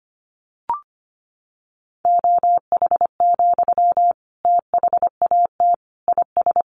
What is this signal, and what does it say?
Morse code, heard as a keyed tone, about 25 wpm, 700 Hz. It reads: OH, THAT IS
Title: Sentences from Top 1100 Words 25wpm